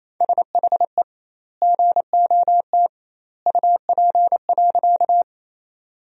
Morse code, heard as a keyed tone, about 28 wpm, 700 Hz. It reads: SHE GOT UP.
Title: 2x Word-Spacing - Sentences from Top 300 Words - 28wpm